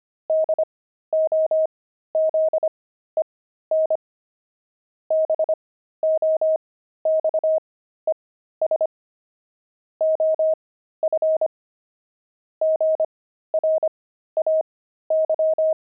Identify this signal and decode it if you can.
Morse code, heard as a keyed tone, about 25 wpm, spaced as for 13 wpm, 630 Hz. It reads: DOZEN BOXES OF GRAY
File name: pangram-fs-13-cs-25-nine